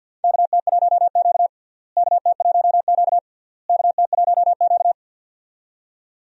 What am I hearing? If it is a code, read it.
Morse code, heard as a keyed tone, about 52 wpm, 700 Hz. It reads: KT1Q KT1Q KT1Q
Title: USA Call Signs - 4 Characters - Repeated 3x - 50wpm